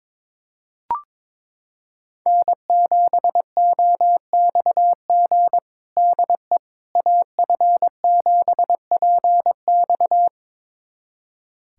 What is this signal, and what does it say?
Morse code, heard as a keyed tone, about 22 wpm, 700 Hz. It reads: N7OXG DE AF7PX
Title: USA Callsign DE Callsign - 5 Characters - 22wpm